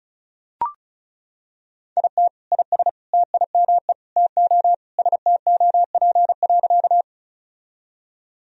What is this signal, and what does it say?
Morse code, heard as a keyed tone, about 36 wpm, 700 Hz. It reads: IT IS TIME TO STOP.
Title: Sentences from Top 1000 Words - Review Part 1 - 35wpm